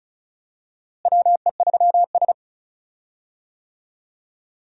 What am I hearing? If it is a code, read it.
Morse code, heard as a keyed tone, about 36 wpm, 700 Hz. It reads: WE3S
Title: USA Call Signs - 4 Characters - Repeated 2x - 35wpm